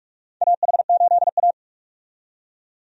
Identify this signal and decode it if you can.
Morse code, heard as a keyed tone, about 46 wpm, 700 Hz. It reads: AH8A
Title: USA Call Signs - 4 Characters - 45wpm